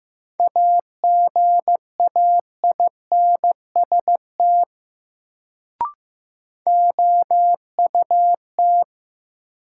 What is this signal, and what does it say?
Morse code, heard as a keyed tone, about 15 wpm, 700 Hz. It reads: AGAINST OUT
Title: Sets of 2 Words - New Word Drill - 15wpm